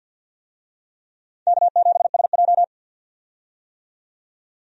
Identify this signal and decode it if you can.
Morse code, heard as a keyed tone, about 53 wpm, 700 Hz. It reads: K7SJ
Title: USA Call Signs - 4 Characters - 50wpm